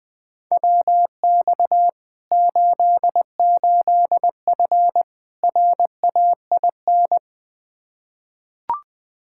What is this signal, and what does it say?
Morse code, heard as a keyed tone, about 21 wpm, 700 Hz. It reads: WX 88F RAIN
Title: Practice QSO - Exchange 2 - Info - 20wpm